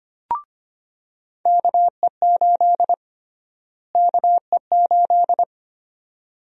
Morse code, heard as a keyed tone, about 25 wpm, 700 Hz. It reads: KE8 KE8
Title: USA Call Sign Prefixes - Repeated 2x - 25wpm